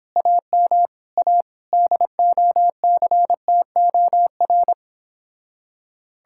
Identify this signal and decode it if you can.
Morse code, heard as a keyed tone, about 29 wpm, 700 Hz. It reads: AM A DOCTOR